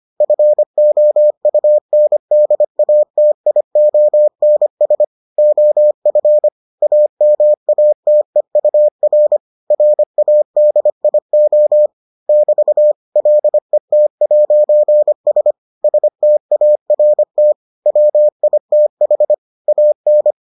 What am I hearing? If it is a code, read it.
Morse code, heard as a keyed tone, about 25 wpm, 600 Hz. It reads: FOUNDATIONS OF AMATEUR RADIO <BT> LET'S START WITH AN